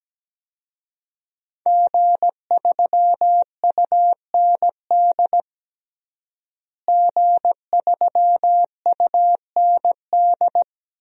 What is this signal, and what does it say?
Morse code, heard as a keyed tone, about 17 wpm, 700 Hz. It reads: G3UND G3UND
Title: DXCC-100 Call Signs - 5 Characters - Repeated 3x - 17wpm